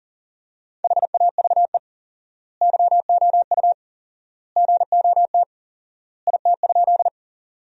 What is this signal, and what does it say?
Morse code, heard as a keyed tone, about 40 wpm, 700 Hz. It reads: HAVE YOU GOT IT?
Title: 4x Word-Spacing - Sentences from Top 300 Words - 40wpm